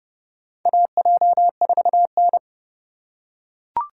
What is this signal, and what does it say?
Morse code, heard as a keyed tone, about 30 wpm, 700 Hz. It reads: AJ4N
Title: USA Call Signs - 4 Characters - 4x Character Spacing - 30wpm